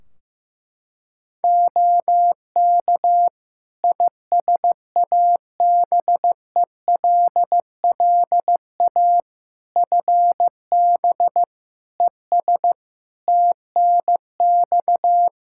Morse code, heard as a keyed tone, about 15 wpm, 700 Hz. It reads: OK ISABELLA FB ES TNX
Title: Practice QSO - Exchange 2 - Info - 15wpm